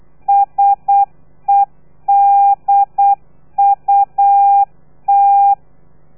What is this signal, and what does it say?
Morse code, heard as a keyed tone, about 8 wpm, 800 Hz. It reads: SEDUT
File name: telegrafia / i2tfj / Lez3 35